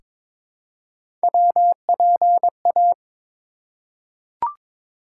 Morse code, heard as a keyed tone, about 21 wpm, 700 Hz. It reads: WPA
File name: ARRL Sections v2 - 22wpm